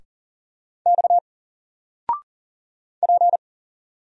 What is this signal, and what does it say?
Morse code, heard as a keyed tone, about 41 wpm, 700 Hz. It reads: X P
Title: Single Letter-Number 40wpm